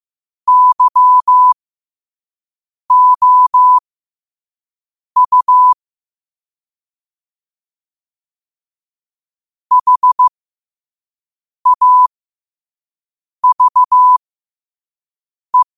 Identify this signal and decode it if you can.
Morse code, heard as a keyed tone, about 15 wpm, spaced as for 5 wpm, 1000 Hz. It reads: YOU HAVE
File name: qotdfiles / 5wpm